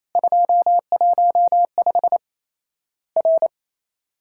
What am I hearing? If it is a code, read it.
Morse code, heard as a keyed tone, about 28 wpm, 700 Hz. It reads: 215 R
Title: Worked All Europe DX Contest - QTC Exchange - 28wpm